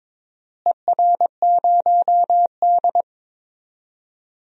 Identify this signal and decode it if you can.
Morse code, heard as a keyed tone, about 22 wpm, 700 Hz. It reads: ER0D
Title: DXCC-100 Call Signs - 4 Characters - Repeated 3x - 22wpm